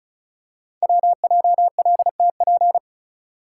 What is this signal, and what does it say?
Morse code, heard as a keyed tone, about 36 wpm, 700 Hz. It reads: WJLTP
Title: Code Groups - 5 Letters - 35wpm